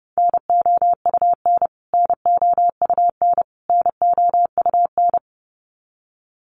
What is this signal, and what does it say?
Morse code, heard as a keyed tone, about 30 wpm, 700 Hz. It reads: NOUN NOUN NOUN
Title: Top 500 Words Repeated 30wpm